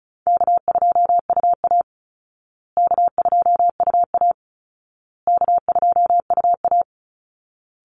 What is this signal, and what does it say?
Morse code, heard as a keyed tone, about 34 wpm, 700 Hz. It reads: K2UA K2UA K2UA
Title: Call Signs Repeated 2x - 35wpm